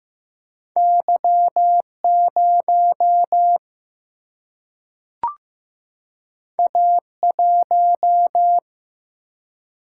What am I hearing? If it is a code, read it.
Morse code, heard as a keyed tone, about 15 wpm, 700 Hz. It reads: Y 0 A 1
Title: Single-Letter Single-Number 15wpm